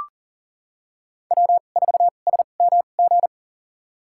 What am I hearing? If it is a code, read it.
Morse code, heard as a keyed tone, about 40 wpm, spaced as for 29 wpm, 700 Hz. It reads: W4SMG
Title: USA Call Signs - 5 Characters - 2x Character Spacing - 40wpm